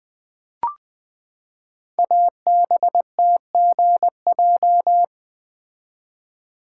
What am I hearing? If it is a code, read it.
Morse code, heard as a keyed tone, about 20 wpm, 700 Hz. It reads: ABTGJ
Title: Code Groups - 5 Letters - 20wpm